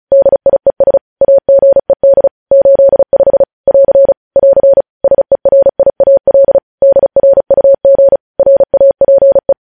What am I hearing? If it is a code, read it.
Morse code, heard as a keyed tone, about 36 wpm, 550 Hz. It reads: DIES AGED 85 <AR> <AR> SERIAL DRUG RAPE
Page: news headlines in morse code at 35 wpm